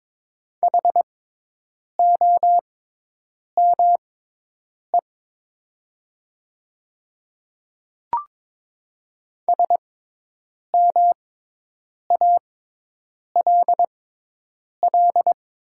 Morse code, heard as a keyed tone, about 22 wpm, spaced as for 8 wpm, 700 Hz. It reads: HOME SMALL
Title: 6x Character Spacing - Top 300 Words - 22wpm